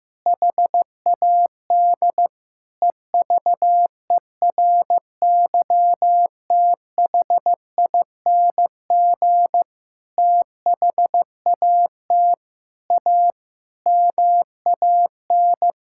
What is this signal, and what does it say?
Morse code, heard as a keyed tone, about 16 wpm, 700 Hz. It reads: HAD EVERYTHING THAT A MAN
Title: Sentences from Top 1000 Words - Review Part 5 - 15wpm